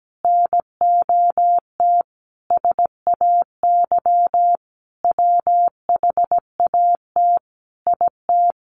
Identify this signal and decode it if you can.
Morse code, heard as a keyed tone, about 17 wpm, 700 Hz. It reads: NOT SAY WHAT IT
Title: Sentences from Top 600 Words 17wpm